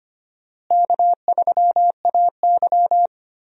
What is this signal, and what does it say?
Morse code, heard as a keyed tone, about 25 wpm, 700 Hz. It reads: K3AY
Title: USA Call Signs - 4 Characters - Repeated 2x - 25wpm